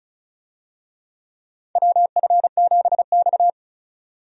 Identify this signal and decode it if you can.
Morse code, heard as a keyed tone, about 36 wpm, 700 Hz. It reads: WF7X
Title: USA Call Signs - 4 Characters - 35wpm